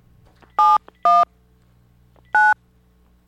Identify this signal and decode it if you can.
DTMF, heard as telephone touch tones, tones of about 210 ms. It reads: 71 9